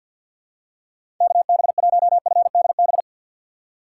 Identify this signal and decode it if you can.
Morse code, heard as a keyed tone, about 52 wpm, 700 Hz. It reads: KB1FDB
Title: USA Call Signs - 6 Characters - 3x Character Spacing - 50wpm